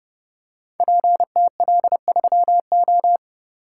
Morse code, heard as a keyed tone, about 30 wpm, 700 Hz. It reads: PTL3O
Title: Code Groups - 5 Letter:Numbers - 30wpm